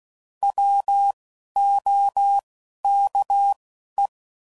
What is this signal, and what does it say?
Morse code, heard as a keyed tone, about 16 wpm, 800 Hz. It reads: W O K E